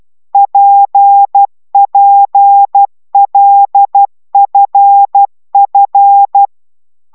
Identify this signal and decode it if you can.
Morse code, heard as a keyed tone, about 12 wpm, 800 Hz. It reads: PPLFF
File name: taviro tanulas / cw 69 10 10